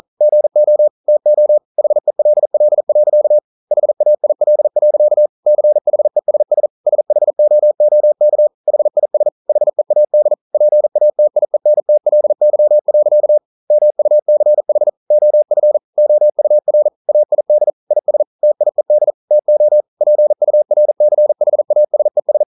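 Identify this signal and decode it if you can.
Morse code, heard as a keyed tone, about 42 wpm, 600 Hz. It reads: GO TO HELL. HAIL. CHESS SHOOK HIS HEAD PATIENTLY. MUCH OF OUR AID IS TIED TO PURCHASES
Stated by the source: Morse code practice at 41 wpm, from CW med Gnister.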